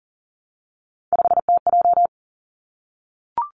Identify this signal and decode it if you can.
Morse code, heard as a keyed tone, about 40 wpm, 700 Hz. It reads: 5TJ
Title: Three Challenging Letter-Number Combinations 40wpm